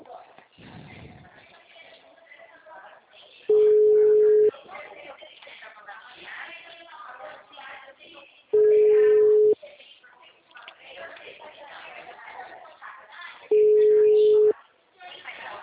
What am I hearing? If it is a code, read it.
Ringback tone (Europe, 425 Hz), with speech.